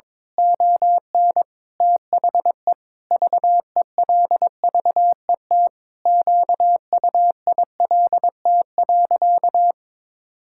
Morse code, heard as a keyed tone, about 22 wpm, 700 Hz. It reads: ON THE VELVET QUILT.
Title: Pangram Challenge v2 - 22wpm